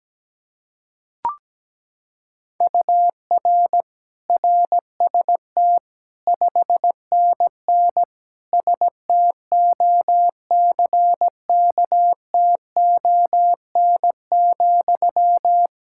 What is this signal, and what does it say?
Morse code, heard as a keyed tone, about 17 wpm, 700 Hz. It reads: UR RST 5NN STOCKTON,
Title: K1USN SST QSO Practice 17wpm